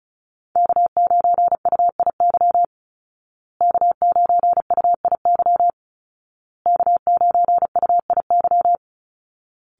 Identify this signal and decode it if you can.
Morse code, heard as a keyed tone, about 34 wpm, 700 Hz. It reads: K9UIY K9UIY K9UIY
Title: Call Signs Repeated 2x - 35wpm